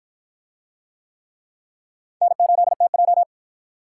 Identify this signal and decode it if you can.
Morse code, heard as a keyed tone, about 50 wpm, 700 Hz. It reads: N8TJ